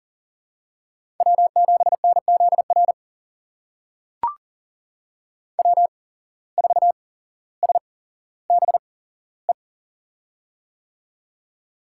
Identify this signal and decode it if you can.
Morse code, heard as a keyed tone, about 40 wpm, spaced as for 17 wpm, 700 Hz. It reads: W7NZR W4SBE
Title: USA Call Signs - 5 Characters - 8x Character Spacing - 40wpm